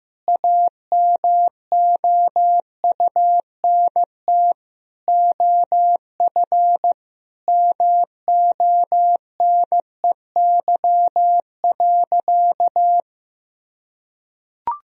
Morse code, heard as a keyed tone, about 16 wpm, 700 Hz. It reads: AMOUNT OF MONEY.
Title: Sentences from Top 1000 Words - Review Part 4 - 15wpm